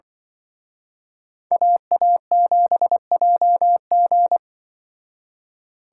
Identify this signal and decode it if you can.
Morse code, heard as a keyed tone, about 24 wpm, 700 Hz. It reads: AA7JG
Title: Speed Racing - USA 2x2 Call Signs 20wpm